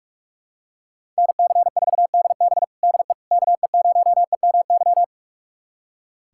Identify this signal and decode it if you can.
Morse code, heard as a keyed tone, about 46 wpm, 700 Hz. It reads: NC4DB DE KE0EMY
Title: USA Callsign DE Callsign 45wpm